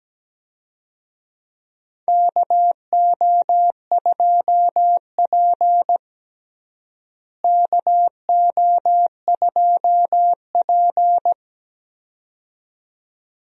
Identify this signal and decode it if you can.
Morse code, heard as a keyed tone, about 17 wpm, 700 Hz. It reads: KO2P KO2P
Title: USA Call Signs - 4 Characters - Repeated 2x - 17wpm